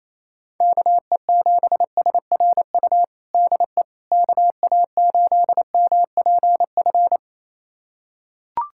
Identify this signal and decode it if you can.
Morse code, heard as a keyed tone, about 28 wpm, 700 Hz. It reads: KE7SRU DE KA8MPF
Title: USA Callsign DE Callsign 28wpm